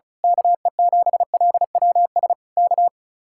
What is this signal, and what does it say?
Morse code, heard as a keyed tone, about 33 wpm, 700 Hz. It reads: KE7LWS K
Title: Speed-Racing Common QSO Phrases - 35wpm